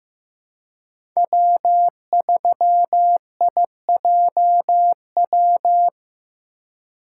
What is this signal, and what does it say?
Morse code, heard as a keyed tone, about 15 wpm, 700 Hz. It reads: W3IJW
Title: USA Call Signs - 5 Characters - 3x Character Spacing - 15wpm